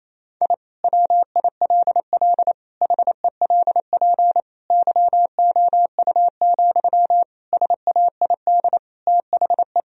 Morse code, heard as a keyed tone, about 27 wpm, 700 Hz. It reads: I WILL HELP YOU, SAID THE
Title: Sentences from Top 1000 Words - Review Part 3 - 28wpm